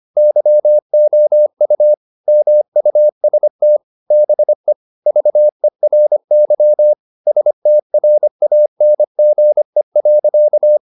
Morse code, heard as a keyed tone, about 25 wpm, 600 Hz. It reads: YOU MUST BE VERY STRANGE.
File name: the old man and the sea part 1 25wpm